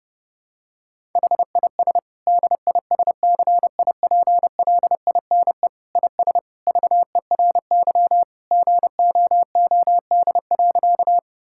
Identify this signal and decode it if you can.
Morse code, heard as a keyed tone, about 30 wpm, 700 Hz. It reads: HIS DISCIPLINE IS VERY GOOD.
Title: Sentences from Top 1000 Words - Review Part 6 - 30wpm